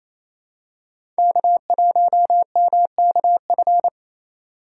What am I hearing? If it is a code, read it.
Morse code, heard as a keyed tone, about 28 wpm, 700 Hz. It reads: K1MKF
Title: USA Call Signs - 5 Characters - 2x Character Spacing - 28wpm